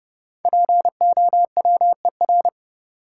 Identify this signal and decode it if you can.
Morse code, heard as a keyed tone, about 30 wpm, 700 Hz. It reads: POWER
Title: Sets of 4 Words - New Word Drill - 30wpm